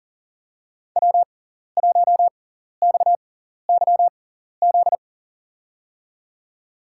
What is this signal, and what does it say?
Morse code, heard as a keyed tone, about 38 wpm, spaced as for 14 wpm, 700 Hz. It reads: W1XYZ
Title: USA Call Signs - 5 Characters - 6x Character Spacing - 40wpm